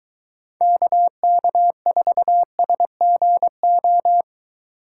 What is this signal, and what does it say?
Morse code, heard as a keyed tone, about 22 wpm, 700 Hz. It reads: KK4SGO